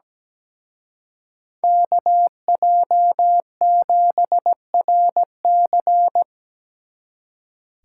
Morse code, heard as a keyed tone, about 20 wpm, 700 Hz. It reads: KJ7RC
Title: Speed Racing - USA Call Signs 17wpm